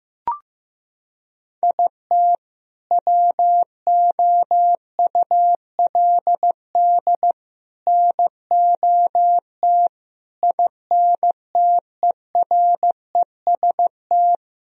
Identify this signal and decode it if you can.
Morse code, heard as a keyed tone, about 15 wpm, 700 Hz. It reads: IT WOULD NOT INTEREST
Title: Sentences from Top 1000 Words - Review Part 2 - 15wpm